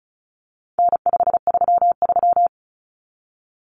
Morse code, heard as a keyed tone, about 34 wpm, 700 Hz. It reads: N533
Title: Common Cut Numbers - 4-Digit 35wpm